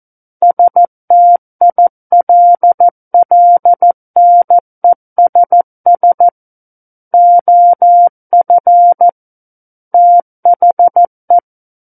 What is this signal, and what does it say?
Morse code, heard as a keyed tone, about 14 wpm, 700 Hz. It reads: STILLNESS OF THE
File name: Book One / 05-Chapter 5 - 14 WPM